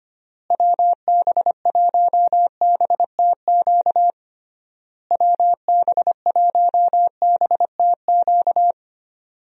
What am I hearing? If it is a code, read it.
Morse code, heard as a keyed tone, about 25 wpm, 700 Hz. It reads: WB1BTQ WB1BTQ
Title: USA Call Signs - 6 Characters - Repeated 3x - 25wpm